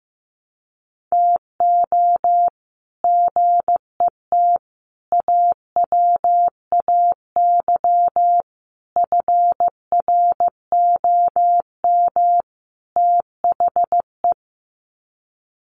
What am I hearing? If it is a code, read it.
Morse code, heard as a keyed tone, about 15 wpm, 700 Hz. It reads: TO GET AWAY FROM THE
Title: Sets of 5 Words Encore2 15wpm